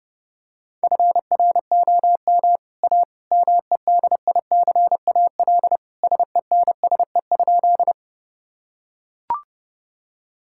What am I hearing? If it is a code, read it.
Morse code, heard as a keyed tone, about 32 wpm, 700 Hz. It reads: FROM A MEDICAL SENSE?
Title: Sentences from Top 1000 Words - Review Part 5 - 30wpm